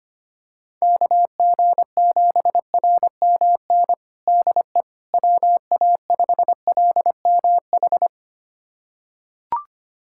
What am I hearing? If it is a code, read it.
Morse code, heard as a keyed tone, about 25 wpm, 700 Hz. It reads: KG7RMN DE WA5LMH
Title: USA Callsign DE Callsign - 6 Characters - 25wpm